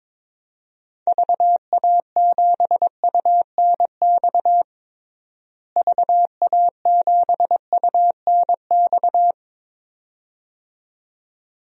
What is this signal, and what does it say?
Morse code, heard as a keyed tone, about 22 wpm, 700 Hz. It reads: VA7UNX VA7UNX
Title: DXCC-100 Call Signs - 6 Characters - Repeated 2x - 22wpm